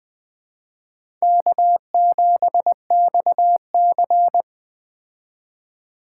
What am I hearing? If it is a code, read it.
Morse code, heard as a keyed tone, about 21 wpm, 700 Hz. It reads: K7XC
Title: Speed Racing - USA 1x2 Call Signs 17wpm